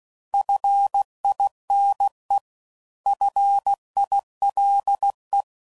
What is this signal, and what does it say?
Morse code, heard as a keyed tone, about 16 wpm, 800 Hz. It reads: FINE FILE
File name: Prefixes / By First Letter / Fs